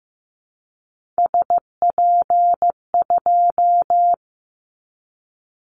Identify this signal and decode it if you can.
Morse code, heard as a keyed tone, about 15 wpm, 700 Hz. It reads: SP2
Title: Callsign Prefixes for Top 25 DXCC - 15wpm